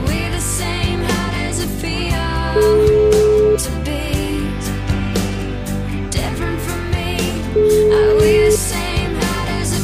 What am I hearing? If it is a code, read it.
Ringback tone (Europe, 425 Hz), with noise.